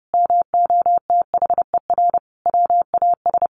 Morse code, heard as a keyed tone, about 30 wpm, 700 Hz. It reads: MOTHER WAS
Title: Sentences from Top 1500 Words 30wpm